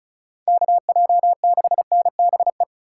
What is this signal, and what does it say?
Morse code, heard as a keyed tone, about 43 wpm, 700 Hz. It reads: KJ6NBE
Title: Speed Racing - USA 2x3 Call Signs 35wpm